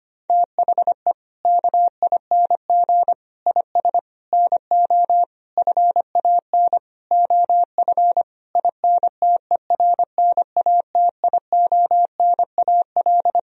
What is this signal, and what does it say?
Morse code, heard as a keyed tone, about 25 wpm, 700 Hz. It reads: THE KING IS NO FAN OF INTERNATIONAL
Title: Sentences from Top 1000 Words - Review Part 4 - 25wpm